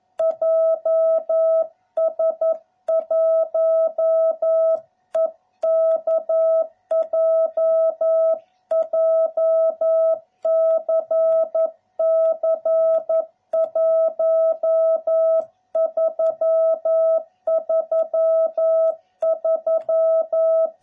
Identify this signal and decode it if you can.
Morse code, heard as a keyed tone, about 11 wpm, 660 Hz. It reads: JS1EKJ JCC 1333